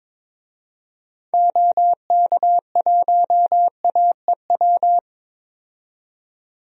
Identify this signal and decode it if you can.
Morse code, heard as a keyed tone, about 22 wpm, 700 Hz. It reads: OK1AEW
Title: DXCC-100 Call Signs - 6 Characters - 22wpm